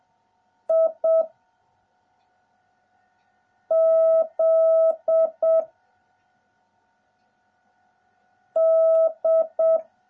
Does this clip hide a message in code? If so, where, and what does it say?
Yes: Morse code from 0.7 to 9.8 s, heard as a keyed tone, about 7 wpm, 650 Hz. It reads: I Z D